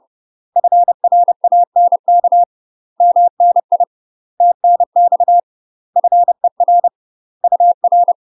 Morse code, heard as a keyed tone, about 30 wpm, 700 Hz. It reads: FRANK MNI TNX FER UR
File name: UK MASTER QSOs TRAINING / 30WPM / UK MASTER QSOs 0038